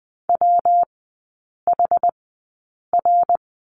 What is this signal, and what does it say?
Morse code, heard as a keyed tone, about 20 wpm, spaced as for 8 wpm, 700 Hz. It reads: WHR
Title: 114 Lesson 19e - Letter w - Three Characters 20wpm